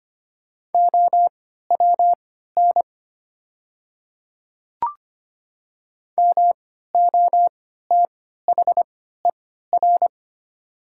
Morse code, heard as a keyed tone, about 25 wpm, spaced as for 14 wpm, 700 Hz. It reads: OWN MOTHER
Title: 3x Character Spacing - Top 300 Words - 25wpm